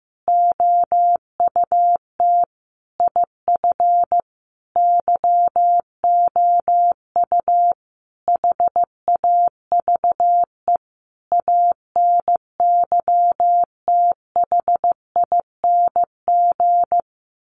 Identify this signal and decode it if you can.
Morse code, heard as a keyed tone, about 15 wpm, 700 Hz. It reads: OUT IF YOU HAVE ANYTHING
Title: Sentences from Top 1300 Words 15wpm